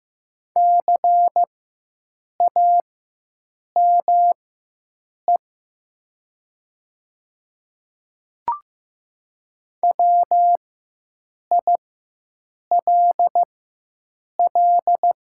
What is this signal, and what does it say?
Morse code, heard as a keyed tone, about 15 wpm, spaced as for 7 wpm, 700 Hz. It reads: CAME WILL
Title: 4x Character Spacing - Top 200 Words - 15wpm